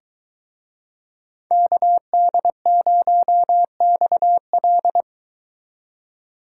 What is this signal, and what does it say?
Morse code, heard as a keyed tone, about 23 wpm, 700 Hz. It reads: KD0XL